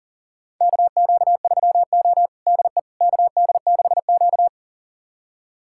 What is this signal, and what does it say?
Morse code, heard as a keyed tone, about 37 wpm, 700 Hz. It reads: KQ3O DE KD6Q
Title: USA Callsign DE Callsign - 4 Characters - 40wpm